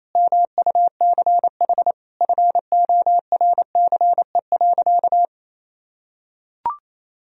Morse code, heard as a keyed tone, about 28 wpm, 700 Hz. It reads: MUCH FORCE.
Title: Sentences from Top 1000 Words - Review Part 3 - 28wpm